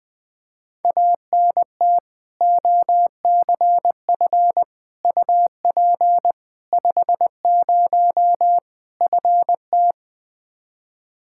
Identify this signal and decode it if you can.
Morse code, heard as a keyed tone, about 21 wpm, 700 Hz. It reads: ANT OCF UP 50 FT
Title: Practice QSO - Exchange 2 - Info - 20wpm